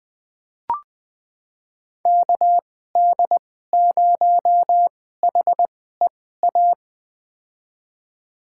Morse code, heard as a keyed tone, about 20 wpm, spaced as for 15 wpm, 700 Hz. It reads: KD0HEA
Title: USA Call Signs - 6 Characters - 2x Character Spacing - 20wpm